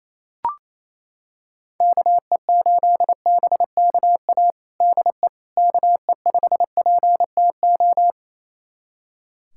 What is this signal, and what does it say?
Morse code, heard as a keyed tone, about 28 wpm, 700 Hz. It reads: KE8BKA DE KE5PTO
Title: USA Callsign DE Callsign - 6 Characters - 28wpm